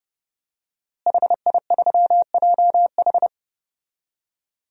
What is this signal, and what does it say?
Morse code, heard as a keyed tone, about 30 wpm, 700 Hz. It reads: HI3JH
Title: DXCC-100 Call Signs - 5 Characters - Repeated 2x - 30wpm